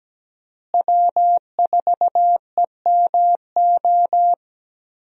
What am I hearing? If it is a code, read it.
Morse code, heard as a keyed tone, about 17 wpm, 700 Hz. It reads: W4EMO
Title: USA Call Signs - 5 Characters - Repeated 2x - 17wpm